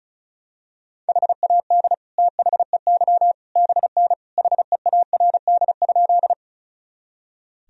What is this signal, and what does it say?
Morse code, heard as a keyed tone, about 36 wpm, 700 Hz. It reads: HAD THEY BN HEARD?
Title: Sentences from Top 400 Words and Abbreviations 35wpm